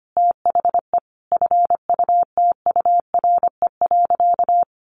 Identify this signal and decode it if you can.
Morse code, heard as a keyed tone, about 25 wpm, 700 Hz. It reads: THE FUTURE.
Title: Sentences from Top 1200 Words 25wpm